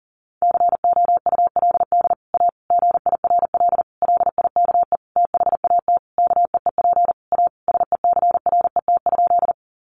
Morse code, heard as a keyed tone, about 40 wpm, 700 Hz. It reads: COULD A GIRL LIKE THAT KEEP A SECRET?
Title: Sentences from Top 1200 Words 40wpm